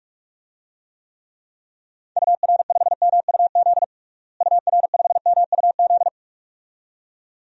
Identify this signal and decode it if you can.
Morse code, heard as a keyed tone, about 43 wpm, 700 Hz. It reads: UR5MUZ UR5MUZ
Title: DXCC-100 Call Signs - 6 Characters - Repeated 2x - 45wpm